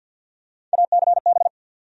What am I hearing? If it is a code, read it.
Morse code, heard as a keyed tone, about 53 wpm, 700 Hz. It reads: ACB
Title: USA Call Sign Postfixes - 3 Letters 50wpm